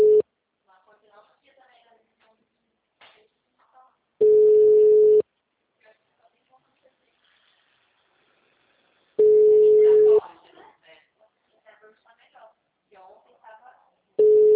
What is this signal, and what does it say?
Ringback tone (Europe, 425 Hz), with speech.
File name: out- -2003- - - .129